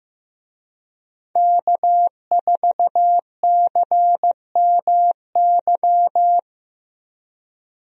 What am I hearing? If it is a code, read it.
Morse code, heard as a keyed tone, about 15 wpm, 700 Hz. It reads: K4CMY
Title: USA Call Signs - 5 Characters - Repeated 2x - 15wpm